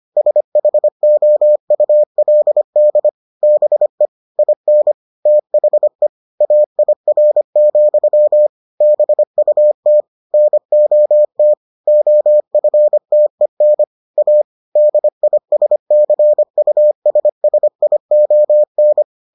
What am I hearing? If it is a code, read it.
Morse code, heard as a keyed tone, about 25 wpm, 600 Hz. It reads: SHOULD BE IN THE AIR, BUT NOT OFTEN A DISCUSSION